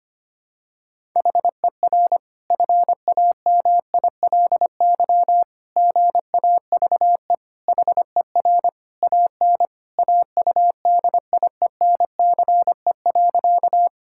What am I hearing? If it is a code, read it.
Morse code, heard as a keyed tone, about 25 wpm, 700 Hz. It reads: HER FAMILY GAVE HER AN AUDIENCE.
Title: Sentences from Top 1000 Words - Review Part 5 - 25wpm